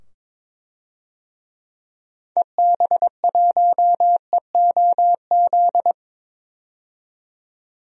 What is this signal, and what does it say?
Morse code, heard as a keyed tone, about 22 wpm, 700 Hz. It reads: EB1EOZ
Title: DXCC-100 Call Signs - 6 Characters - 22wpm